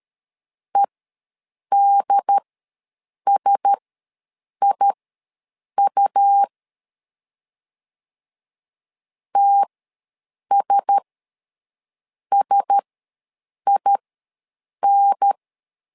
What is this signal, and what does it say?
Morse code, heard as a keyed tone, about 13 wpm, spaced as for 7 wpm, 800 Hz. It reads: EDSIU TSSIN
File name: LEZIONI / 4 Q1S ws 12wpm 1827